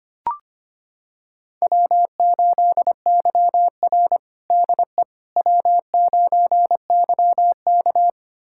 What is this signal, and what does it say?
Morse code, heard as a keyed tone, about 25 wpm, 700 Hz. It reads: W8YR DE W9YK
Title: USA Callsign DE Callsign - 4 Characters - 25wpm